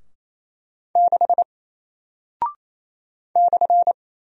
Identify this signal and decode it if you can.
Morse code, heard as a keyed tone, about 27 wpm, 700 Hz. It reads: 6 /
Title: Single Character (Letter:Number:Punctuation:Prosign) 28wpm